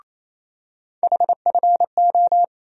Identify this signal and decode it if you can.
Morse code, heard as a keyed tone, about 28 wpm, 700 Hz. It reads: HFO
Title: USA Call Sign Postfixes - 3 Letters 28wpm